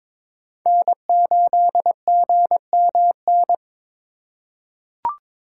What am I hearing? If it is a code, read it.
Morse code, heard as a keyed tone, about 22 wpm, 700 Hz. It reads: N8GMN
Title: USA Call Signs - 5 Characters - 4x Character Spacing - 22wpm